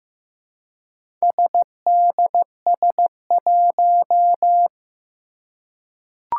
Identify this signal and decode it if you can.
Morse code, heard as a keyed tone, about 15 wpm, 700 Hz. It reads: SDS1
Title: Four Challenging Letter-Number Combinations 15wpm